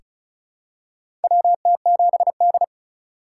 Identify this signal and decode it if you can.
Morse code, heard as a keyed tone, about 36 wpm, 700 Hz. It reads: WT7D
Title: USA Call Signs - 4 Characters - 3x Character Spacing - 35wpm